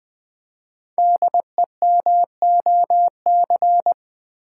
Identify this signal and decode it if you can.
Morse code, heard as a keyed tone, about 20 wpm, 700 Hz. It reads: DEMOC
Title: Build A Word - 5 Letters At A Time - 10-Letter Words - 20wpm